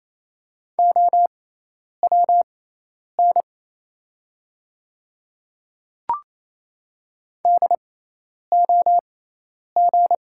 Morse code, heard as a keyed tone, about 28 wpm, spaced as for 10 wpm, 700 Hz. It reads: OWN DOG
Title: 6x Character Spacing - Top 500 Words - 28wpm